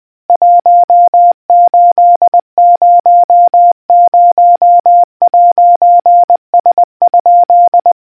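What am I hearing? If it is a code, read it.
Morse code, heard as a keyed tone, about 20 wpm, 700 Hz. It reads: 1800'S?
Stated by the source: Morse quiz question for naval history buffs from the RNZN Communicators Association.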